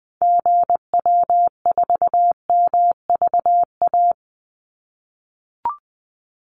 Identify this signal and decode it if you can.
Morse code, heard as a keyed tone, about 20 wpm, 700 Hz. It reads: GW4MVA
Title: Call Signs - 20wpm